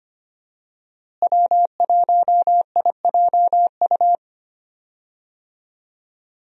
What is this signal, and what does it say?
Morse code, heard as a keyed tone, about 25 wpm, 700 Hz. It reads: W1IJU
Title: USA Call Signs - 5 Characters - Repeated 3x - 25wpm